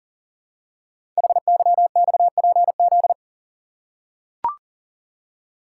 Morse code, heard as a keyed tone, about 42 wpm, 690 Hz. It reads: HYXPZ
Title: Code Groups - 5 Letters - 40wpm